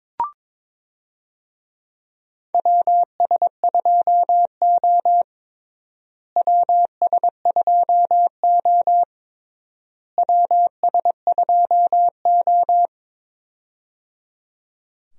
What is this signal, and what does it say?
Morse code, heard as a keyed tone, about 22 wpm, 700 Hz. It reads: WS2O WS2O WS2O
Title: USA Call Signs - 4 Characters - Repeated 3x - 22wpm